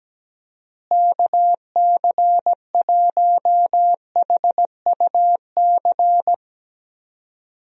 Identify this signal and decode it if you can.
Morse code, heard as a keyed tone, about 17 wpm, 700 Hz. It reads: KC1HUC
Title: USA Call Signs - 6 Characters - Repeated 2x - 17wpm